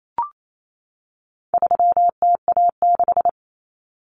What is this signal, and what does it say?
Morse code, heard as a keyed tone, about 28 wpm, 700 Hz. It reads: 3TA6
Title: Common Cut Numbers - 4-Digit 28wpm